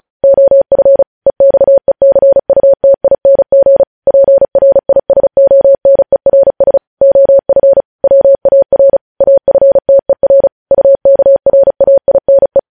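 Morse code, heard as a keyed tone, about 36 wpm, 550 Hz. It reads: OF EXECUTING PRISONERS OF WAR AFTER UKRAINE
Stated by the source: News headlines sent in Morse at 35 wpm.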